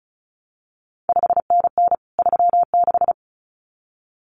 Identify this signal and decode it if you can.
Morse code, heard as a keyed tone, about 36 wpm, 700 Hz. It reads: 5NN 36
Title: CQ WW DX Contest Exchange Encore - 35wpm